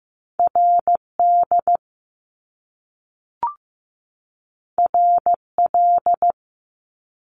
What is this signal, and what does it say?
Morse code, heard as a keyed tone, about 15 wpm, 700 Hz. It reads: RD RL
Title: Top Two Letter Combinations in Words 15wpm